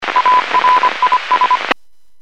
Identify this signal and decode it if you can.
Morse code, heard as a keyed tone, about 25 wpm, 1000 Hz, with radio noise. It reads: ARIS